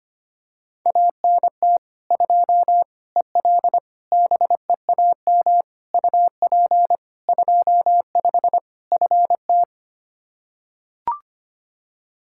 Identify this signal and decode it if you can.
Morse code, heard as a keyed tone, about 25 wpm, 700 Hz. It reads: ANT 2 EL BEAM UP 25 FT
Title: Practice QSO - Exchange 2 - Info - 25wpm